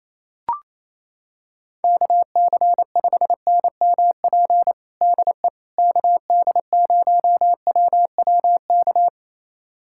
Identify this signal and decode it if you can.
Morse code, heard as a keyed tone, about 28 wpm, 700 Hz. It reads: KC5NMP DE KD0WWK
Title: USA Callsign DE Callsign 28wpm